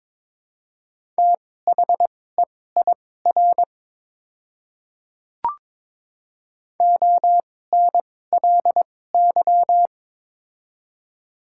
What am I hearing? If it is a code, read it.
Morse code, heard as a keyed tone, about 22 wpm, spaced as for 16 wpm, 700 Hz. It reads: THEIR ONLY
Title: 2x Character Spacing - Top 300 Words - 22wpm